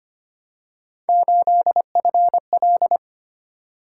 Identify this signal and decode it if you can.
Morse code, heard as a keyed tone, about 25 wpm, 700 Hz. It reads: 8FL
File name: Three Letter-Number Combinations - 25wpm